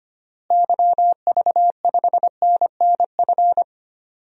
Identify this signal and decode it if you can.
Morse code, heard as a keyed tone, about 25 wpm, 700 Hz. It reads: YV5NNF
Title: DXCC-100 Call Signs - 6 Characters - Repeated 3x - 25wpm